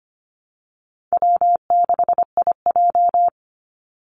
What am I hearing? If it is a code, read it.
Morse code, heard as a keyed tone, about 25 wpm, 700 Hz. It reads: W6IJ
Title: Call Signs Repeated 2x - 25wpm